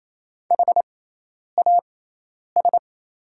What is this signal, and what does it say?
Morse code, heard as a keyed tone, about 28 wpm, spaced as for 10 wpm, 700 Hz. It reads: HAS
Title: 6x Character Spacing - Top 500 Words - 28wpm